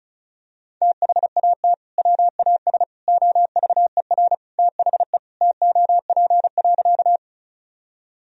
Morse code, heard as a keyed tone, about 34 wpm, 700 Hz. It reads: THAT WAS OVER THE TOP.
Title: Sentences from Top 1000 Words - Review Part 2 - 35wpm